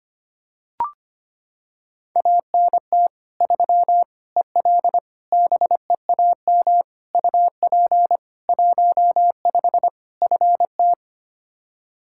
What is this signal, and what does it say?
Morse code, heard as a keyed tone, about 25 wpm, 700 Hz. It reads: ANT 3 EL BEAM UP 15 FT
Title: Practice QSO - Exchange 2 - Info - 25wpm